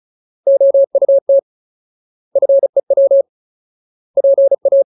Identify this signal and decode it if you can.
Morse code, heard as a keyed tone, about 37 wpm, 550 Hz. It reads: OUT FEW PA
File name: Ultimate Word List / 35WPM Ultimate